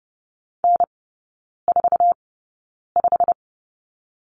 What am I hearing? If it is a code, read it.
Morse code, heard as a keyed tone, about 30 wpm, spaced as for 9 wpm, 700 Hz. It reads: N45
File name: course / 30:10wpm / 087 Lesson 15e - Number 5 - Three Characters 30wpm